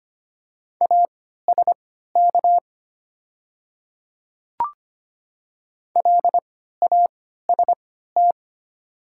Morse code, heard as a keyed tone, about 25 wpm, spaced as for 14 wpm, 700 Hz. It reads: ASK LAST
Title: 3x Character Spacing - Top 500 Words - 25wpm